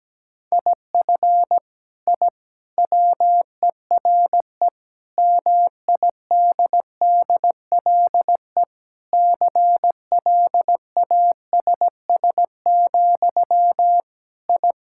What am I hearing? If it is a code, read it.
Morse code, heard as a keyed tone, about 17 wpm, 700 Hz. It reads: IF I WERE MIDDLE CLASS, I
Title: Sentences from Top 1000 Words - Review Part 6 - 17wpm